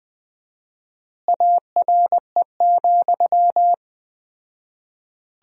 Morse code, heard as a keyed tone, about 20 wpm, 700 Hz. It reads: ARE,
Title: 4x Word-Spacing - Sentences from Top 100 Words - 20wpm